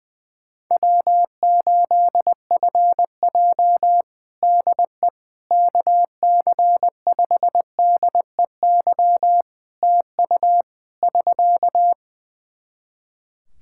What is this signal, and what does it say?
Morse code, heard as a keyed tone, about 21 wpm, 700 Hz. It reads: W8FJ DE KC5DEY TU <SK>
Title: Practice QSO - Exchange 4 - Ending - 20wpm